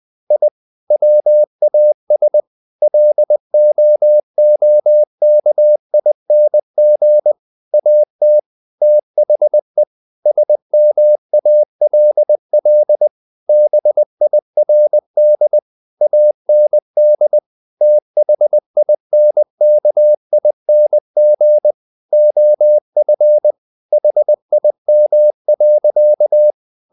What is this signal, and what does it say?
Morse code, heard as a keyed tone, about 20 wpm, 600 Hz. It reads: I WAS LOOKING AT THE SMALL BIRD AND THINKING OF HIM.
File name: the old man and the sea / the old man and the sea part 1 20wpm